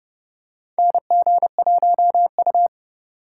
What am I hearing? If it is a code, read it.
Morse code, heard as a keyed tone, about 31 wpm, 700 Hz. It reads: NG1U
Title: USA Call Signs - 4 Characters - 3x Character Spacing - 30wpm